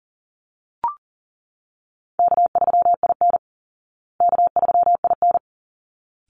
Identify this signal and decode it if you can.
Morse code, heard as a keyed tone, about 40 wpm, 700 Hz. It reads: K3IN K3IN
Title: Easy Call Signs Repeated 2x - 40wpm